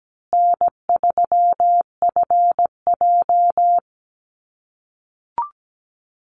Morse code, heard as a keyed tone, about 17 wpm, 700 Hz. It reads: N3FJ
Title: USA Call Signs Repeated 3x 17wpm